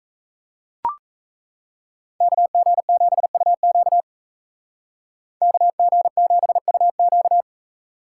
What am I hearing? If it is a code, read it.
Morse code, heard as a keyed tone, about 37 wpm, 700 Hz. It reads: KG7UQ KG7UQ